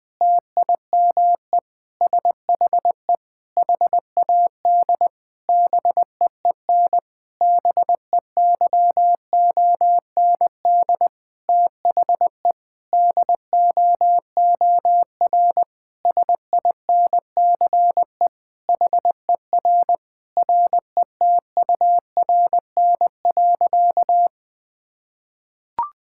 Morse code, heard as a keyed tone, about 20 wpm, 700 Hz. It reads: TIME SHE HAD BEEN BEYOND THE DOOR SINCE HER RETURN.
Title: Sentences from Top 1000 Words - Review Part 6 - 20wpm